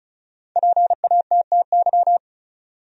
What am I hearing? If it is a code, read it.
Morse code, heard as a keyed tone, about 36 wpm, 700 Hz. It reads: PATTY
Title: USA Common Names - Max 5 Letters - 35wpm